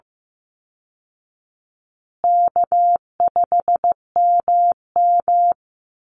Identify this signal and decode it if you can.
Morse code, heard as a keyed tone, about 15 wpm, 700 Hz. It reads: K5MM
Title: Call Signs Repeated 2x - 15wpm